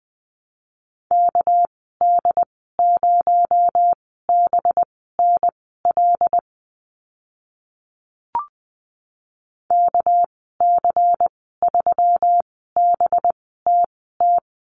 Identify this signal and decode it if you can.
Morse code, heard as a keyed tone, about 20 wpm, spaced as for 15 wpm, 700 Hz. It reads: KD0BNL KC3BTT
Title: 245 Lesson 39g - Letter z - Call Signs 20wpm